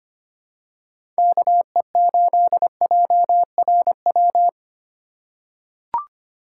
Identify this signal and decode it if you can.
Morse code, heard as a keyed tone, about 25 wpm, 700 Hz. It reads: KE8JRW
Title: USA Call Signs - 6 Characters - 2x Character Spacing - 25wpm